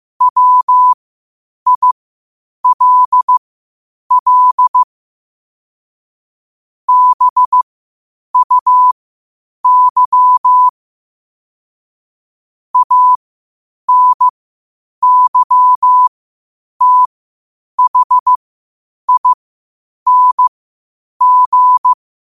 Morse code, heard as a keyed tone, about 15 wpm, spaced as for 8 wpm, 1000 Hz. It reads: WILL BUY ANYTHING
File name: qotdfiles / 8wpm